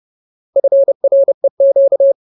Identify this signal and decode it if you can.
Morse code, heard as a keyed tone, about 30 wpm, 550 Hz. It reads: FREQ
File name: Ultimate Word List / 30WPM Ultimate